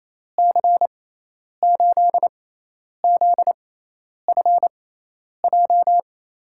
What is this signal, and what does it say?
Morse code, heard as a keyed tone, about 28 wpm, spaced as for 10 wpm, 700 Hz. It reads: C8ZFJ
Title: USA Call Signs - 6 Characters - 6x Character Spacing - 28wpm